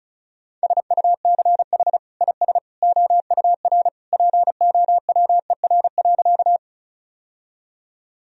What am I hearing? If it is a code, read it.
Morse code, heard as a keyed tone, about 34 wpm, 700 Hz. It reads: SUCH IS OUR POWER.
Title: Sentences from Top 1000 Words - Review Part 2 - 35wpm